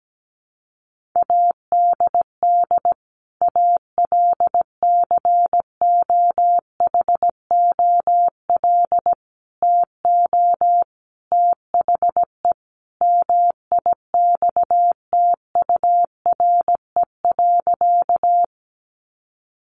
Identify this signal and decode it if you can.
Morse code, heard as a keyed tone, about 17 wpm, 700 Hz. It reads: ADD ALCOHOL TO THE MIXTURE.
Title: Sentences from Top 1500 Words 17wpm